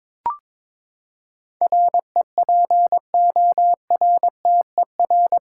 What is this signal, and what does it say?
Morse code, heard as a keyed tone, about 22 wpm, 700 Hz. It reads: REPORTER
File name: practice / 22wpm / Eight Letter Words - 22wpm